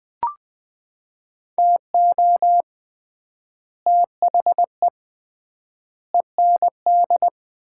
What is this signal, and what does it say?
Morse code, heard as a keyed tone, about 20 wpm, 700 Hz. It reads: TO THE END
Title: 3x Word-Spacing - Sets of 4 Words - 20wpm